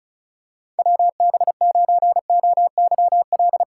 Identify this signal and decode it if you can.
Morse code, heard as a keyed tone, about 36 wpm, 700 Hz. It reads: WB9OYL
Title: USA Callsign DE Callsign - 6 Characters - 35wpm